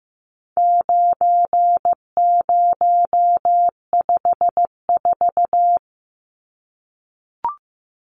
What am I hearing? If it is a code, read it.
Morse code, heard as a keyed tone, about 15 wpm, 700 Hz. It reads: 9054
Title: Numbers - 4-Digit 15wpm